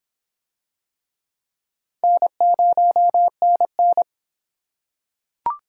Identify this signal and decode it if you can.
Morse code, heard as a keyed tone, about 25 wpm, 700 Hz. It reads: N0NN